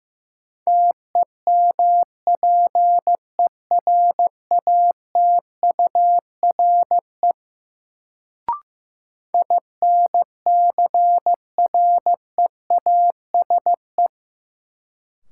Morse code, heard as a keyed tone, about 15 wpm, 700 Hz. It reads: TEMPERATURE INCREASE
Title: Sentences from Top 1000 Words - Review Part 4 - New Word Drill - 15wpm